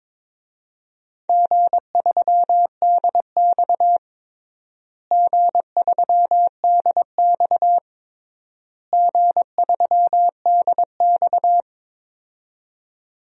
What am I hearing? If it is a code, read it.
Morse code, heard as a keyed tone, about 22 wpm, 700 Hz. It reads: G3DX G3DX G3DX
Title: DXCC-100 Call Signs - 4 Characters - Repeated 3x - 22wpm